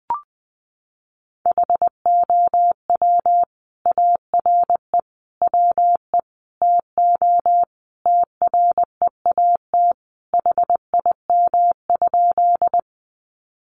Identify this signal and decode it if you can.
Morse code, heard as a keyed tone, about 20 wpm, 700 Hz. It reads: HOW ARE WE TO TREAT HIM?
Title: Sentences from Top 1300 Words 20wpm